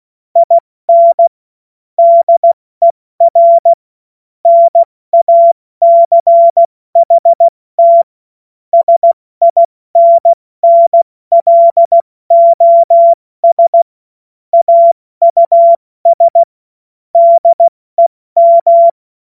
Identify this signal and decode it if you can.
Morse code, heard as a keyed tone, about 16 wpm, spaced as for 14 wpm, 680 Hz. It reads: IN DER NACHT SINNLOS AUS DEM
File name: Morsekurs/Witze